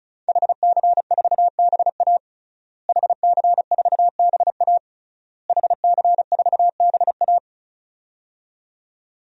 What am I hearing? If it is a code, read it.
Morse code, heard as a keyed tone, about 34 wpm, 700 Hz. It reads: HC4BA HC4BA HC4BA
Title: DXCC-100 Call Signs - 5 Characters - Repeated 3x - 35wpm